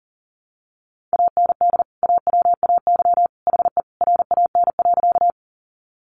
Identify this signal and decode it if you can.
Morse code, heard as a keyed tone, about 40 wpm, 700 Hz. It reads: AND AWAY HE RAN.
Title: Sentences from Top 600 Words 40wpm